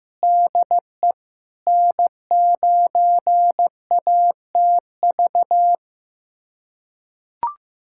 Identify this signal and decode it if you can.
Morse code, heard as a keyed tone, about 16 wpm, 700 Hz. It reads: DE N9ATV
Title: USA Callsign DE Callsign - 5 Characters - 15wpm